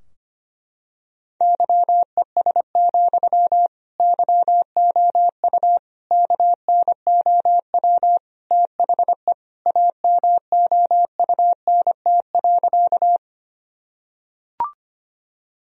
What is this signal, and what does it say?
Morse code, heard as a keyed tone, about 25 wpm, 700 Hz. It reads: YES, YOU KNOW THE AMOUNT.
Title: Sentences from Top 1000 Words - Review Part 4 - 25wpm